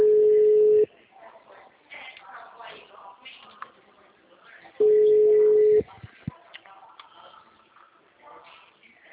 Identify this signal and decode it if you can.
Ringback tone (Europe, 425 Hz), with speech.
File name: out- -2005- - - .89839